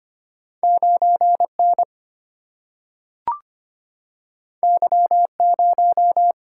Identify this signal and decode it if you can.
Morse code, heard as a keyed tone, about 25 wpm, 700 Hz. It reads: 9N Y0
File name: Two Letter-Number Combinations - 25wpm